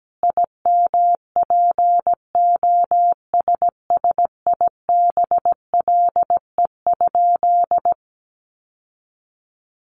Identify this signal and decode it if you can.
Morse code, heard as a keyed tone, about 17 wpm, 700 Hz. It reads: IMPOSSIBLE?
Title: Sentences from Top 1100 Words 17wpm